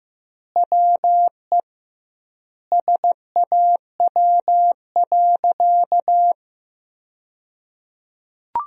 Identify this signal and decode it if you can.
Morse code, heard as a keyed tone, about 15 wpm, 700 Hz. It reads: WE SAW.
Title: 2x Word-Spacing - Sentences from Top 300 Words - 15wpm